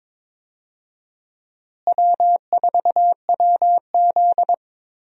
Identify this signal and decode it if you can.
Morse code, heard as a keyed tone, about 22 wpm, 700 Hz. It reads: W4WZ
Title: USA Call Signs - 4 Characters - Repeated 3x - 22wpm